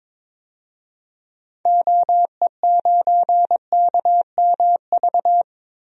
Morse code, heard as a keyed tone, about 22 wpm, 700 Hz. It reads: OE9KMV
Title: DXCC-100 Call Signs - 6 Characters - Repeated 3x - 22wpm